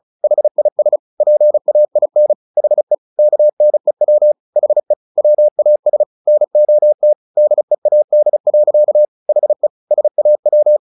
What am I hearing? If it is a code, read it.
Morse code, heard as a keyed tone, about 34 wpm, 600 Hz. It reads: HIS PAIN HE KNEW HE WAS NOT DEAD. HE SAW
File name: the old man and the sea 2 35wpm